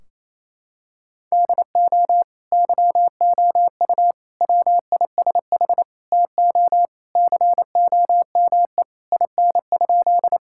Morse code, heard as a keyed tone, about 27 wpm, 700 Hz. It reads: DO YOU WISH TO COME IN?
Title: Sentences from Top 1600 Words 28wpm